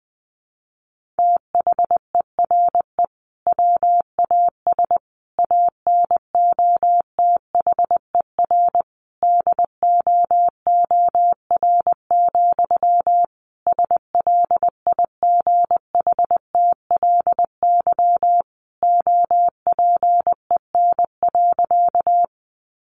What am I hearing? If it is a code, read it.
Morse code, heard as a keyed tone, about 20 wpm, 700 Hz. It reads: THERE WAS ANOTHER DOOR, SLIGHTLY OPEN.
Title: Sentences from Top 1300 Words 20wpm